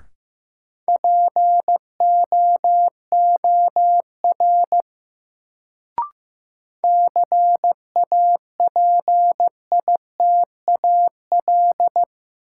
Morse code, heard as a keyed tone, about 15 wpm, 700 Hz. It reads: POOR CAPITAL
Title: Sentences from Top 1000 Words - New Word Drill - 15wpm